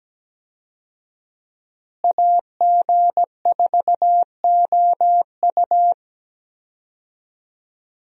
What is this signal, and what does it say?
Morse code, heard as a keyed tone, about 17 wpm, 700 Hz. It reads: AG4OU
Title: USA Call Signs - 5 Characters - Repeated 3x - 17wpm